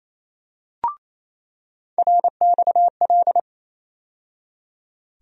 Morse code, heard as a keyed tone, about 28 wpm, 700 Hz. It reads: RXL
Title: USA Call Sign Postfixes - 3 Letters 28wpm